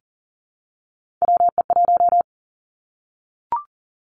Morse code, heard as a keyed tone, about 39 wpm, 700 Hz. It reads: WE1
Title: Three Challenging Letter-Number Combinations 40wpm